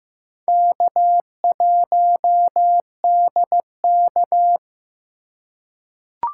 Morse code, heard as a keyed tone, about 15 wpm, 700 Hz. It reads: K1DK
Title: USA Call Signs - 4 Characters - 2x Character Spacing - 15wpm